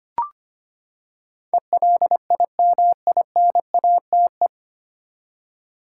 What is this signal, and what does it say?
Morse code, heard as a keyed tone, about 25 wpm, 700 Hz. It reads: ELIMINATE
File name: Nine Letter Words - 25wpm